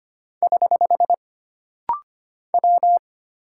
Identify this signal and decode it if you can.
Morse code, heard as a keyed tone, about 25 wpm, 700 Hz. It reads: <HH> W
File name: Single Character (Letter:Number:Punctuation:Prosign) - Rapid-Fire - 25wpm